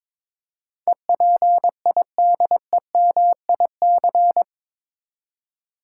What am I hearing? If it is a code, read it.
Morse code, heard as a keyed tone, about 22 wpm, 700 Hz. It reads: EPIDEMIC
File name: practice / 22wpm / Eight Letter Words - 22wpm